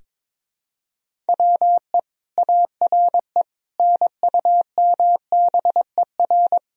Morse code, heard as a keyed tone, about 22 wpm, 700 Hz. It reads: WE ARE NUMBER
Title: Sentences from Top 1000 Words - Review Part 1 - 22wpm